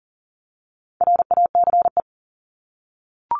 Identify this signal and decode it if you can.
Morse code, heard as a keyed tone, about 40 wpm, 700 Hz. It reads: RACE
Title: Top 1501-1750 Words 40wpm